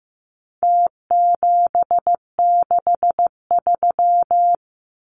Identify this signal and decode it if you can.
Morse code, heard as a keyed tone, about 15 wpm, 700 Hz. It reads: T763
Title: Numbers - Contest Style - 4-Digit 15wpm